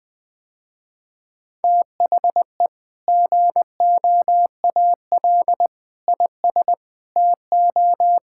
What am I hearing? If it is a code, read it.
Morse code, heard as a keyed tone, about 21 wpm, 700 Hz. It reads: THE GOAL IS TO
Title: Sentences with 2-Word Phrases - Spoken Beginning - 20wpm